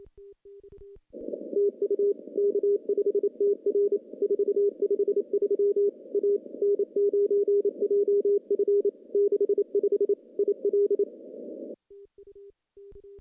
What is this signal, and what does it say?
Morse code, heard as a keyed tone, about 32 wpm, 400 Hz. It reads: TU K5TR 453 A N9JF 65 IL
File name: CW-SS-k5tr